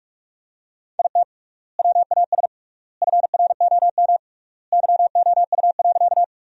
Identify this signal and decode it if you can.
Morse code, heard as a keyed tone, about 47 wpm, 700 Hz. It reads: IT WAS FROM YOU.
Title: 3x Word-Spacing - Sentences from Top 100 Words - 45wpm